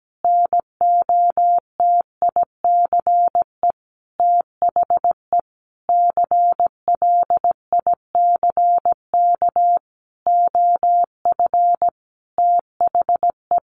Morse code, heard as a keyed tone, about 17 wpm, 700 Hz. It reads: NOTICE THE CLICK OF THE
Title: Sentences from Top 1100 Words 17wpm